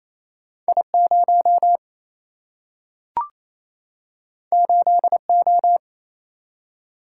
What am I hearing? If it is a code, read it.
Morse code, heard as a keyed tone, about 28 wpm, 700 Hz. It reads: I0 8O
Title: Single-Letter Single-Number 28wpm